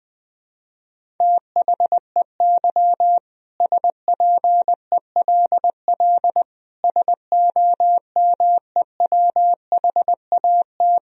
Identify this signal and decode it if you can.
Morse code, heard as a keyed tone, about 20 wpm, 700 Hz. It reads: THEY SPELL SOMEWHAT
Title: Sentences from Top 1000 Words - Review Part 1 - 20wpm